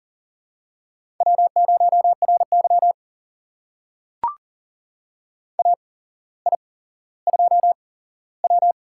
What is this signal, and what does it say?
Morse code, heard as a keyed tone, about 39 wpm, spaced as for 17 wpm, 700 Hz. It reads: W0RY AI2W
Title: USA Call Signs - 4 Characters - 8x Character Spacing - 40wpm